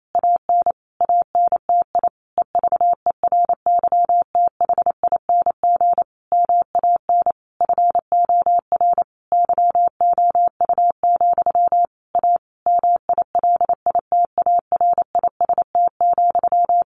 Morse code, heard as a keyed tone, about 28 wpm, 700 Hz. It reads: AN ANTI EVERYTHING MAN FOR YOU, A MILITARIST,
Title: A Sound Of Thunder 28wpm